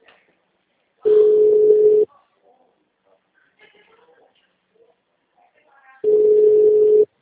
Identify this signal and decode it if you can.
Ringback tone (Europe, 425 Hz), with speech.